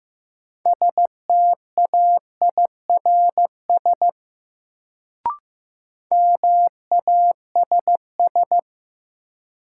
Morse code, heard as a keyed tone, about 15 wpm, 700 Hz. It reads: STAIRS MASS
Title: Sets of 5 Words Encore - New Word Drill - 15wpm